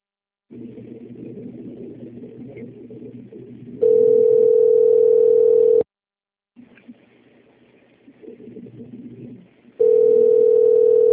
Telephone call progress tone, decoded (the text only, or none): ringback tone